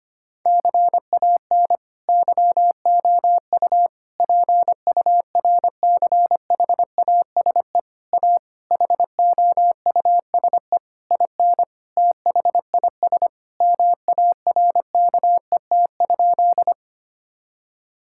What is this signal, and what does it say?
Morse code, heard as a keyed tone, about 25 wpm, 700 Hz. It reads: CAN YOU PURCHASE A HOUSE IN THIS MARKET?
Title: Sentences from Top 1000 Words - Review Part 5 - 25wpm